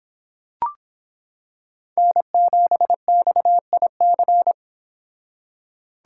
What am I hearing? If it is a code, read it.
Morse code, heard as a keyed tone, about 26 wpm, 700 Hz. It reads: N7XIC